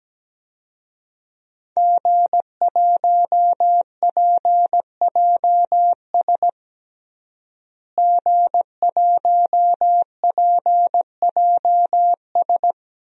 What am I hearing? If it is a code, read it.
Morse code, heard as a keyed tone, about 17 wpm, 700 Hz. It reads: G1PJS G1PJS
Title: DXCC-100 Call Signs - 5 Characters - Repeated 3x - 17wpm